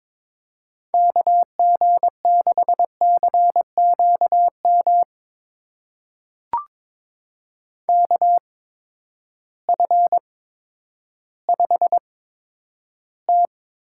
Morse code, heard as a keyed tone, about 21 wpm, spaced as for 11 wpm, 700 Hz. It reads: KG6CQM KF5T
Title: USA Call Signs - 8x Character Spacing - 22wpm